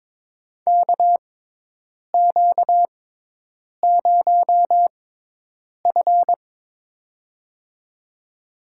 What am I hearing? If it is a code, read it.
Morse code, heard as a keyed tone, about 21 wpm, spaced as for 8 wpm, 700 Hz. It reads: KQ0F
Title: USA Call Signs - 4 Characters - 6x Character Spacing - 22wpm